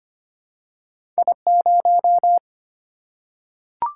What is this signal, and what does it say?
Morse code, heard as a keyed tone, about 25 wpm, 700 Hz. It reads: I0
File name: Two Letter-Number Combinations - 25wpm